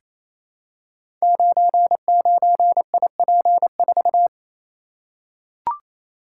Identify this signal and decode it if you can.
Morse code, heard as a keyed tone, about 28 wpm, 700 Hz. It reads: 99IP4
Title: Code Groups - 5 Letter:Numbers - 28wpm